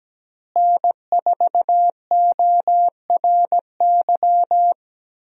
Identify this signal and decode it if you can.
Morse code, heard as a keyed tone, about 17 wpm, 700 Hz. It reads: N4ORY
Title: USA Call Signs - 5 Characters - Repeated 2x - 17wpm